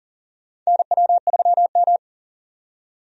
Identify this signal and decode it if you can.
Morse code, heard as a keyed tone, about 40 wpm, 690 Hz. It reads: NW3M
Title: USA Call Signs - 4 Characters - 40wpm